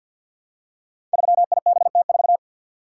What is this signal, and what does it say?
Morse code, heard as a keyed tone, about 47 wpm, 700 Hz. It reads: 3IBT4
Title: Code Groups - 5 Letter:Numbers - 50wpm